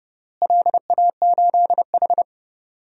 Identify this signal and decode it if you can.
Morse code, heard as a keyed tone, about 30 wpm, 700 Hz. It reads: LA8H
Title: DXCC-100 Call Signs - 4 Characters - Repeated 3x - 30wpm